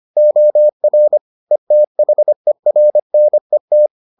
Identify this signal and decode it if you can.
Morse code, heard as a keyed tone, about 25 wpm, 600 Hz. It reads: OR ETHERNET